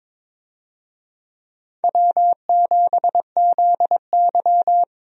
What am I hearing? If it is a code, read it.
Morse code, heard as a keyed tone, about 22 wpm, 700 Hz. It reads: W7ZY
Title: USA Call Signs - 4 Characters - Repeated 3x - 22wpm